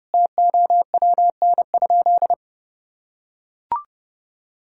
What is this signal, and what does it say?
Morse code, heard as a keyed tone, about 32 wpm, 700 Hz. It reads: TOWN?
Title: Sentences from Top 1000 Words - Review Part 6 - 30wpm